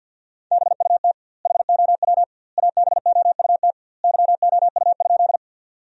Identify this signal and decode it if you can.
Morse code, heard as a keyed tone, about 50 wpm, 700 Hz. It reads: BUT HOW ABOUT YOU?
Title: 2x Word-Spacing - Sentences from Top 100 Words - 50wpm